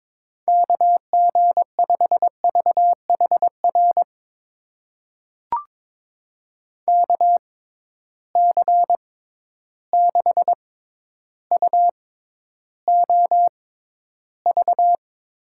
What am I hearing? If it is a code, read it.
Morse code, heard as a keyed tone, about 22 wpm, spaced as for 11 wpm, 700 Hz. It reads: KG5VHR KC6UOV
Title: USA Call Signs - 6 Characters - 6x Character Spacing - 22wpm